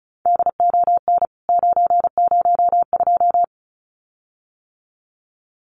Morse code, heard as a keyed tone, about 34 wpm, 700 Hz. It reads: DON 902
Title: CWT Exchange 35wpm